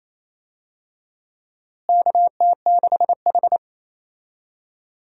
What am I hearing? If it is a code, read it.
Morse code, heard as a keyed tone, about 28 wpm, 700 Hz. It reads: KT6H
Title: USA Call Signs - 4 Characters - Repeated 3x - 28wpm